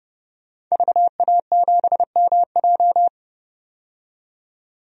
Morse code, heard as a keyed tone, about 31 wpm, 700 Hz. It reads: VA7MJ
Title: DXCC-100 Call Signs - 5 Characters - 30wpm